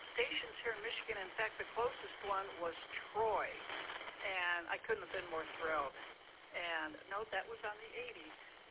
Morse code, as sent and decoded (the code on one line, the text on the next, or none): none